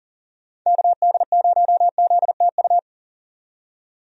Morse code, heard as a keyed tone, about 40 wpm, 700 Hz. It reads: KD0ZTU
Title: USA Call Signs - 6 Characters - 40wpm